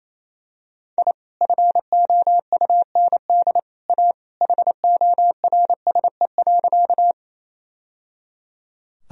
Morse code, heard as a keyed tone, about 27 wpm, 700 Hz. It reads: I FOUND A HORSE.
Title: Sentences from Top 1000 Words - Review Part 1 - 28wpm